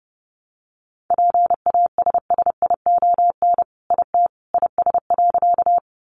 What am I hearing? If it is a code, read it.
Morse code, heard as a keyed tone, about 30 wpm, 700 Hz. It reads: PASSION IT IS.
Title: Sentences from Top 1300 Words 30wpm